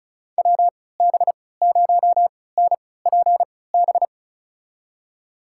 Morse code, heard as a keyed tone, about 36 wpm, spaced as for 20 wpm, 700 Hz. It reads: WB0NPB
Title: USA Call Signs - 6 Characters - 3x Character Spacing - 35wpm